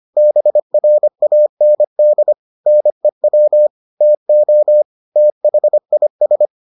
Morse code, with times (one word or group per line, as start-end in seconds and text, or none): 0.2-2.3 BRAND
2.7-3.7 NEW
4.0-4.8 TO
5.2-6.5 THIS